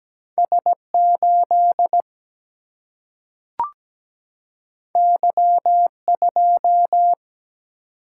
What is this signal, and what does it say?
Morse code, heard as a keyed tone, about 17 wpm, 700 Hz. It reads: S 8 Y 2
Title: Single-Letter Single-Number 17wpm